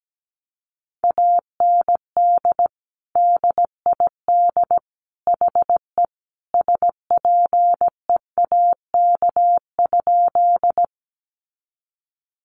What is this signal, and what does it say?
Morse code, heard as a keyed tone, about 17 wpm, 700 Hz. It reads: AND DID HE SPEAK?
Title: Sentences from Top 1100 Words 17wpm